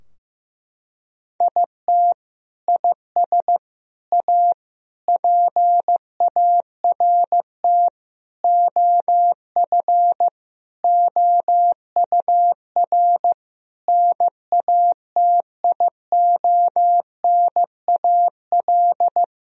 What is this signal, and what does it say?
Morse code, heard as a keyed tone, about 16 wpm, 700 Hz. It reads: IT IS A PART OF OUR NATIONAL
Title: Sentences from Top 1000 Words - Review Part 4 - 15wpm